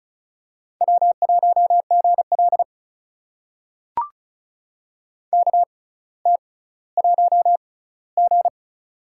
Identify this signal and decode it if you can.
Morse code, heard as a keyed tone, about 36 wpm, spaced as for 18 wpm, 700 Hz. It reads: W1GL KT1G
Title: USA Call Signs - 4 Characters - 6x Character Spacing - 35wpm